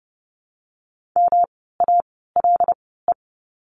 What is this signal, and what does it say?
Morse code, heard as a keyed tone, about 30 wpm, spaced as for 17 wpm, 700 Hz. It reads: MALE
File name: course / 30:10wpm / 108 Lesson 18f - Letter m - Words 30wpm